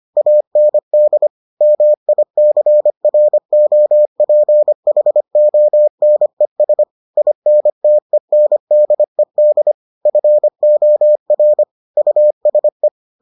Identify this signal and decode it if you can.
Morse code, heard as a keyed tone, about 25 wpm, 600 Hz. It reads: AND MICROPHONES INTENDED FOR USE